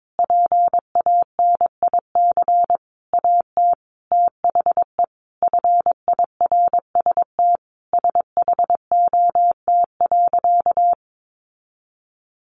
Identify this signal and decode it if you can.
Morse code, heard as a keyed tone, about 22 wpm, 700 Hz. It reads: PANIC AT THE FIRST SHOT.
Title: A Sound Of Thunder 22wpm